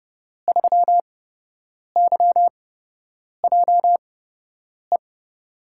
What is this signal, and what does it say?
Morse code, heard as a keyed tone, about 32 wpm, spaced as for 8 wpm, 700 Hz. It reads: 3YJE
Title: USA Call Signs - 5 Characters - 8x Character Spacing - 30wpm